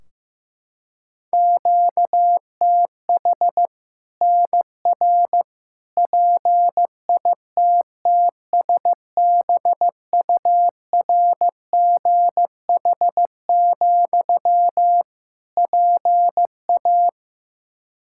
Morse code, heard as a keyed tone, about 15 wpm, 700 Hz. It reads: QTH NR PITTSBURGH, PA
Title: Practice QSO - Exchange 1 - Beginning - 15wpm